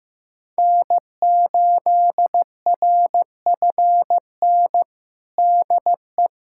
Morse code, heard as a keyed tone, about 15 wpm, 700 Hz. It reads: N8RFN DE
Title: USA Callsign DE Callsign - 5 Characters - 15wpm